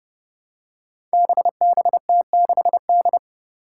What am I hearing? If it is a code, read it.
Morse code, heard as a keyed tone, about 30 wpm, 700 Hz. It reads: BBT6D
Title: Five Challenging Letter-Number Combinations 30wpm